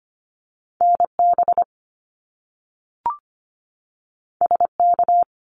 Morse code, heard as a keyed tone, about 25 wpm, 700 Hz. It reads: NB SK
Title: Canadian Provinces and Territories 25wpm